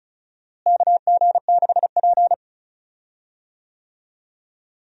Morse code, heard as a keyed tone, about 34 wpm, 700 Hz. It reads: KG6P
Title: USA Call Signs - 4 Characters - Repeated 2x - 35wpm